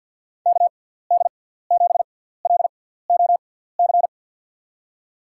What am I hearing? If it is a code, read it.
Morse code, heard as a keyed tone, about 53 wpm, spaced as for 17 wpm, 700 Hz. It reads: KD7LOC
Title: USA Call Signs - 6 Characters - 6x Character Spacing - 50wpm